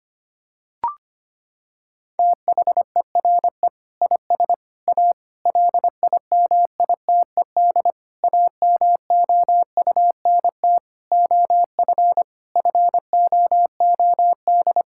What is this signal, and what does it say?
Morse code, heard as a keyed tone, about 25 wpm, 700 Hz. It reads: THERE IS A LIMITED AMOUNT OF FOOD
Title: Sentences from Top 1000 Words - Review Part 6 - 25wpm